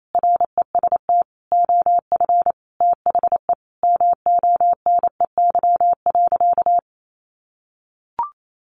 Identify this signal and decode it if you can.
Morse code, heard as a keyed tone, about 28 wpm, 700 Hz. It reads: REST OF THE MONEY.
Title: Sentences from Top 1500 Words 28wpm